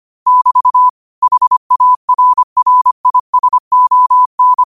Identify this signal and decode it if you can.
Morse code, heard as a keyed tone, about 25 wpm, 1000 Hz. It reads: <BT> HARRISON
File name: qotdfiles / 25wpm